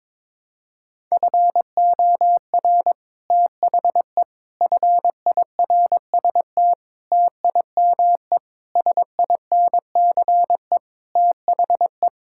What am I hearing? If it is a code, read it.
Morse code, heard as a keyed tone, about 22 wpm, 700 Hz. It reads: FOR THE FIRST TIME SINCE THE
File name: practice / 22wpm / Sets of 6 Words - 22wpm